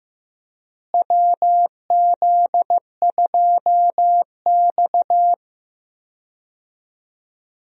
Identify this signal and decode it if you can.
Morse code, heard as a keyed tone, about 15 wpm, 700 Hz. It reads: WZ2X
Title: USA Call Signs - 4 Characters - Repeated 2x - 15wpm